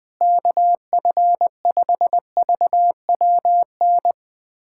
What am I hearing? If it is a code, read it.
Morse code, heard as a keyed tone, about 21 wpm, 700 Hz. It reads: KF5VWN
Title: USA Callsign DE Callsign - 6 Characters - 20wpm